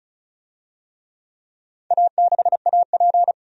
Morse code, heard as a keyed tone, about 34 wpm, 700 Hz. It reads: A6AP
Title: DXCC-100 Call Signs - 4 Characters - 35wpm